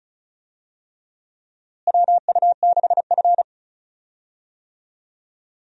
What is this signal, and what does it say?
Morse code, heard as a keyed tone, about 36 wpm, 700 Hz. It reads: WU6F
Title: USA Call Signs - 4 Characters - Repeated 3x - 35wpm